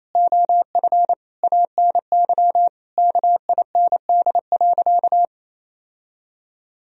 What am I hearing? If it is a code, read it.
Morse code, heard as a keyed tone, about 28 wpm, 700 Hz. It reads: OF ANY KIND.
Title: Sentences from Top 1000 Words - Review Part 4 - 28wpm